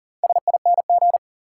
Morse code, heard as a keyed tone, about 40 wpm, 700 Hz. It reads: SING
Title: Four Letter Words - 40wpm